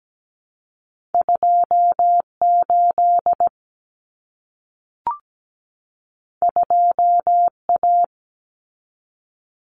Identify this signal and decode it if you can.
Morse code, heard as a keyed tone, about 17 wpm, 700 Hz. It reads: 28 2A
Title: Common Cut Numbers - 2-Digit 17wpm